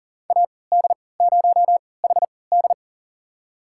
Morse code, heard as a keyed tone, about 39 wpm, spaced as for 22 wpm, 700 Hz. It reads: AD0HD
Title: USA Call Signs - 5 Characters - 3x Character Spacing - 40wpm